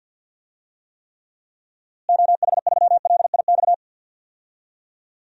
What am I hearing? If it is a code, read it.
Morse code, heard as a keyed tone, about 52 wpm, 700 Hz. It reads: OH3LIX
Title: DXCC-100 Call Signs - 6 Characters - Repeated 2x - 50wpm